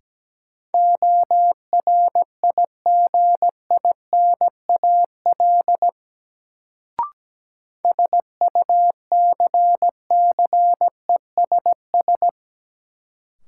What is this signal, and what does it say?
Morse code, heard as a keyed tone, about 17 wpm, 700 Hz. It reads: ORIGINAL SUCCESS
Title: Sentences from Top 900 Words - New Word Drill - 17wpm